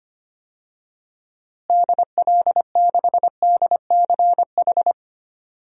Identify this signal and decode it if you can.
Morse code, heard as a keyed tone, about 25 wpm, 700 Hz. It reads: DL6DCH
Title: DXCC-100 Call Signs - 6 Characters - Repeated 3x - 25wpm